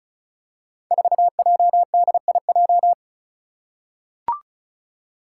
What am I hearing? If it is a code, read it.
Morse code, heard as a keyed tone, about 36 wpm, 700 Hz. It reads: 4JDIJ
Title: Five Challenging Letter-Number Combinations 35wpm